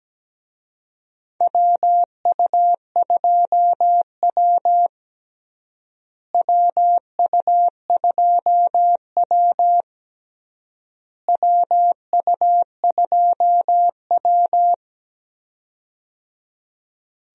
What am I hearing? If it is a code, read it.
Morse code, heard as a keyed tone, about 17 wpm, 700 Hz. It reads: WU2W WU2W WU2W
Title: USA Call Signs - 4 Characters - Repeated 3x - 17wpm